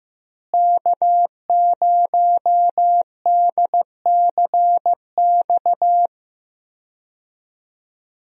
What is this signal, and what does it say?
Morse code, heard as a keyed tone, about 15 wpm, 700 Hz. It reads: K0DCX
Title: USA Call Signs - 5 Characters - Repeated 2x - 15wpm